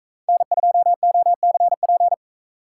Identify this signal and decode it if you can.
Morse code, heard as a keyed tone, about 42 wpm, 700 Hz. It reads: N2OCP